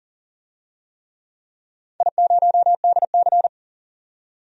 Morse code, heard as a keyed tone, about 40 wpm, 700 Hz. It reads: I0DC
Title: DXCC-100 Call Signs - 4 Characters - 40wpm